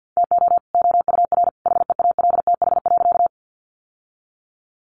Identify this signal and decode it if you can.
Morse code, heard as a keyed tone, about 50 wpm, 700 Hz. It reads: TO OUR HEALTH.
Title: Sentences from Top 600 Words 50wpm